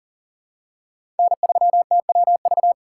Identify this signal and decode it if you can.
Morse code, heard as a keyed tone, about 40 wpm, 700 Hz. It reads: N3TWV
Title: USA Call Signs - 5 Characters - 3x Character Spacing - 40wpm